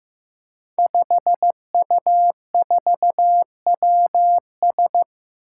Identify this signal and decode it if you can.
Morse code, heard as a keyed tone, about 15 wpm, 700 Hz. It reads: 5U4WS
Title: Five Challenging Letter-Number Combinations 15wpm